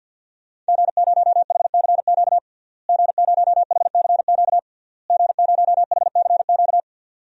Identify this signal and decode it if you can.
Morse code, heard as a keyed tone, about 52 wpm, 700 Hz. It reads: G0HCQ G0HCQ G0HCQ
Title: DXCC-100 Call Signs - 5 Characters - Repeated 3x - 50wpm